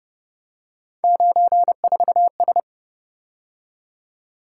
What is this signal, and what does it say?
Morse code, heard as a keyed tone, about 29 wpm, 700 Hz. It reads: 94S